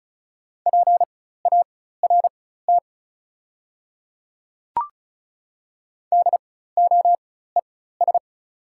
Morse code, heard as a keyed tone, about 35 wpm, spaced as for 16 wpm, 690 Hz. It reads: PART DOES
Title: 4x Character Spacing - Top 200 Words - 35wpm